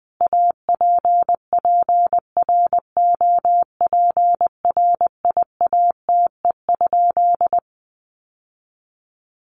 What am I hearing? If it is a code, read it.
Morse code, heard as a keyed tone, about 20 wpm, 700 Hz. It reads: APPROPRIATE?
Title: Sentences from Top 900 Words 20wpm